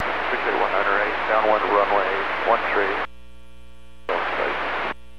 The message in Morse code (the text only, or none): none